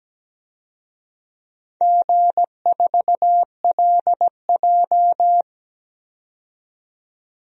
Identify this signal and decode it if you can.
Morse code, heard as a keyed tone, about 17 wpm, 700 Hz. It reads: G4LJ
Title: DXCC-100 Call Signs - 4 Characters - 17wpm